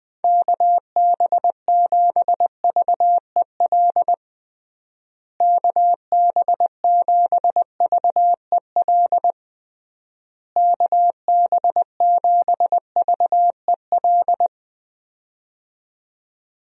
Morse code, heard as a keyed tone, about 20 wpm, 700 Hz. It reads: KB7VEL KB7VEL KB7VEL
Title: USA Call Signs - 6 Characters - Repeated 3x - 20wpm